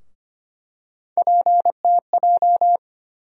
Morse code, heard as a keyed tone, about 25 wpm, 700 Hz. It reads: PTJ
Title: USA Call Sign Postfixes - 3 Letters 25wpm